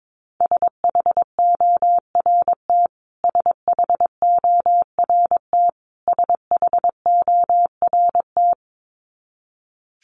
Morse code, heard as a keyed tone, about 22 wpm, 700 Hz. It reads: SHORT SHORT SHORT
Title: Top 500 Words Repeated 22wpm